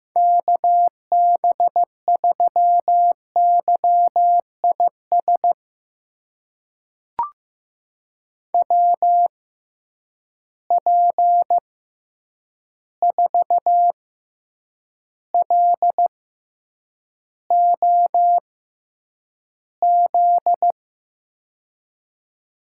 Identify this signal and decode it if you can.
Morse code, heard as a keyed tone, about 16 wpm, spaced as for 8 wpm, 700 Hz. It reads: KB3YIS WP4LOZ
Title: USA Call Signs - 6 Characters - 6x Character Spacing - 15wpm